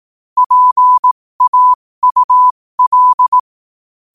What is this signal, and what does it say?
Morse code, heard as a keyed tone, about 18 wpm, spaced as for 15 wpm, 1000 Hz. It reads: PAUL
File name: qotdfiles / 15wpm